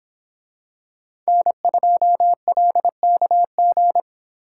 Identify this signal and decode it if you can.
Morse code, heard as a keyed tone, about 26 wpm, 700 Hz. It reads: N2LKG